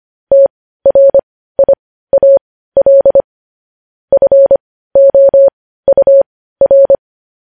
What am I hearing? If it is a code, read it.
Morse code, heard as a keyed tone, about 25 wpm, spaced as for 15 wpm, 550 Hz. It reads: TRIAL FOUR
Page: News Headlines in Morse Code at 15 WPM